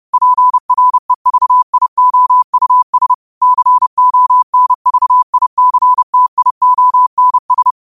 Morse code, heard as a keyed tone, about 30 wpm, 1000 Hz. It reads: PREVIOUS CONVICTIONS